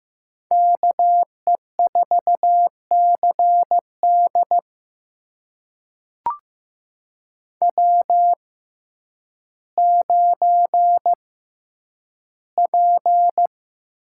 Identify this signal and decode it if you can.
Morse code, heard as a keyed tone, about 15 wpm, spaced as for 9 wpm, 700 Hz. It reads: KE4CD W9P
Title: USA Call Signs - 5 Characters - 6x Character Spacing - 15wpm